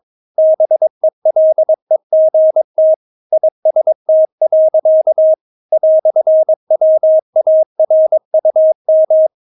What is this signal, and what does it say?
Morse code, heard as a keyed tone, about 22 wpm, 630 Hz. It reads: BELEGT IST. "WARUM
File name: Witz3 22wpm